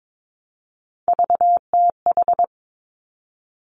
Morse code, heard as a keyed tone, about 22 wpm, 700 Hz. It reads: VTH
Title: Three Challenging Letter-Number Combinations 22wpm